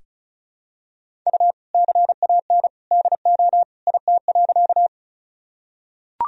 Morse code, heard as a keyed tone, about 34 wpm, 700 Hz. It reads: U CAN DO IT.
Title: Sentences from Top 100 Words and Abbreviations 35wpm